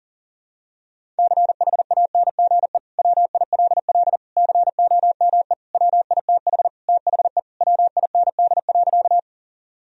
Morse code, heard as a keyed tone, about 40 wpm, 700 Hz. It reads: CHANGE WILL COME WITH THE WIND.
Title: Sentences from Top 1000 Words - Review Part 2 - 40wpm